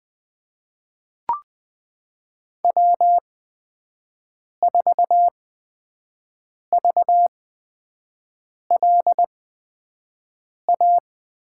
Morse code, heard as a keyed tone, about 20 wpm, spaced as for 5 wpm, 700 Hz. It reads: W4VLA
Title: USA Call Signs - 5 Characters - 8x Character Spacing - 20wpm